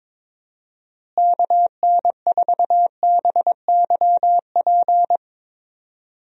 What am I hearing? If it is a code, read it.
Morse code, heard as a keyed tone, about 22 wpm, 700 Hz. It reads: KN4BYP
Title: USA Call Signs - 6 Characters - Repeated 2x - 22wpm